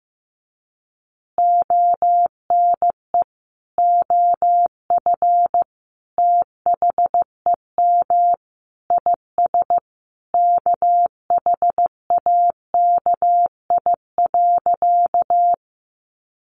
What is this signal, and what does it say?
Morse code, heard as a keyed tone, about 15 wpm, 700 Hz. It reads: ONE OF THEM IS KHAKI.
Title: Additional Easy Phrases 15wpm